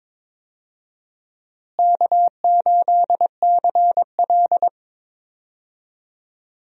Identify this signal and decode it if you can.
Morse code, heard as a keyed tone, about 21 wpm, 700 Hz. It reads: K8CL
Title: USA Call Signs - 4 Characters - Repeated 2x - 22wpm